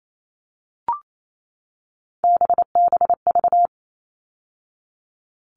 Morse code, heard as a keyed tone, about 28 wpm, 700 Hz. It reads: BBV
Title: All Cut Numbers - 3-Digit 28wpm